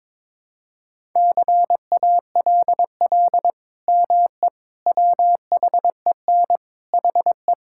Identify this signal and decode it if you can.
Morse code, heard as a keyed tone, about 21 wpm, 700 Hz. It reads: CALL ME WHEN HE
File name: practice / 22wpm / Sentences with 2-Word Phrases - 22wpm